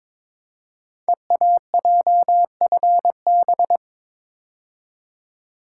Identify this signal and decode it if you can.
Morse code, heard as a keyed tone, about 22 wpm, 700 Hz. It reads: EAJFB
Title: Code Groups - 5 Letter:Numbers - 22wpm